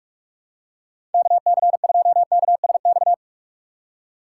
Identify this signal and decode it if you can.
Morse code, heard as a keyed tone, about 47 wpm, 700 Hz. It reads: KC2KSX
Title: USA Call Signs - 6 Characters - 4x Character Spacing - 45wpm